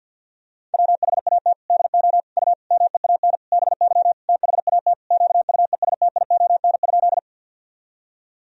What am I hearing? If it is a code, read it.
Morse code, heard as a keyed tone, about 52 wpm, 700 Hz. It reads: WHAT DO U MEAN BY THAT QUESTION?
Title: Sentences from Top 400 Words and Abbreviations 50wpm